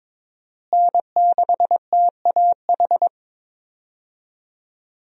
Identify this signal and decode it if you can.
Morse code, heard as a keyed tone, about 22 wpm, 700 Hz. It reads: N6TAH
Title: USA Call Signs - 5 Characters - Repeated 2x - 22wpm